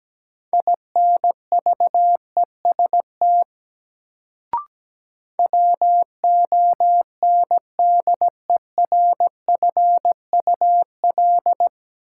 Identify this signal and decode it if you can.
Morse code, heard as a keyed tone, about 17 wpm, 700 Hz. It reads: INVEST WONDERFUL
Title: Sentences from Top 1100 Words - New Word Drill Alt - 17wpm